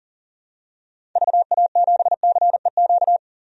Morse code, heard as a keyed tone, about 40 wpm, 700 Hz. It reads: VA7CEQ
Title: DXCC-100 Call Signs - 6 Characters - 40wpm